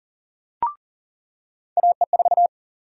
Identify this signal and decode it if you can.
Morse code, heard as a keyed tone, about 40 wpm, 700 Hz. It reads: AE4
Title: USA Call Sign Prefixes - 40wpm